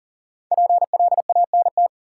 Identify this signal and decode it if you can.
Morse code, heard as a keyed tone, about 38 wpm, 700 Hz. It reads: PLANT
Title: Sentences from Top 1000 Words - Review Part 1 - New Word Drill - 40wpm